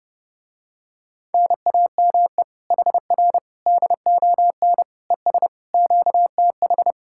Morse code, heard as a keyed tone, about 29 wpm, 700 Hz. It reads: NAME HR DON ES QTH